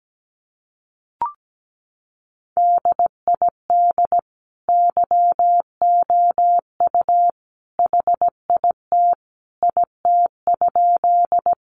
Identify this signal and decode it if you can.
Morse code, heard as a keyed tone, about 17 wpm, 700 Hz. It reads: DID YOU HIT IT?
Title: Sentences from Top 1200 Words 17wpm